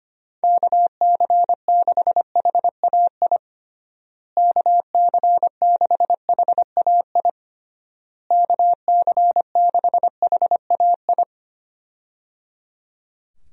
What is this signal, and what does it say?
Morse code, heard as a keyed tone, about 25 wpm, 700 Hz. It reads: KC6HAI KC6HAI KC6HAI
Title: USA Call Signs - 6 Characters - Repeated 3x - 25wpm